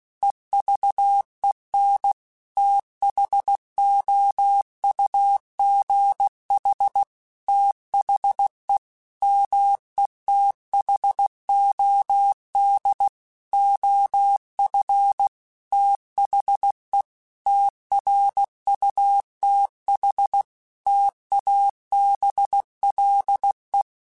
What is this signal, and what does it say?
Morse code, heard as a keyed tone, about 16 wpm, 800 Hz. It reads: EVEN THOUGH THE METHOD OF THE TRUTH TABLE
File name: Technical Subjects / Technical Subjects-mp3s / AND Gate @15